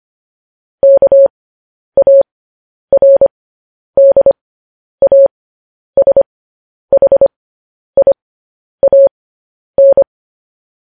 Morse code, heard as a keyed tone, about 25 wpm, spaced as for 10 wpm, 550 Hz. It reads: KARDASHIAN
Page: news headlines in morse code at 10 wpm